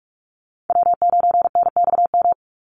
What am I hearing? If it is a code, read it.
Morse code, heard as a keyed tone, about 44 wpm, 700 Hz. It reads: W9NXM
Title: Call Signs Repeated 2x - 45wpm